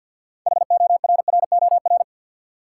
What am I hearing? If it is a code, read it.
Morse code, heard as a keyed tone, about 47 wpm, 700 Hz. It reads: HORROR
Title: Six Letter Words - 50wpm